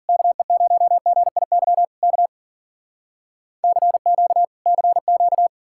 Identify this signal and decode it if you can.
Morse code, heard as a keyed tone, about 58 wpm, 700 Hz. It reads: KE0GIY K CQ CQ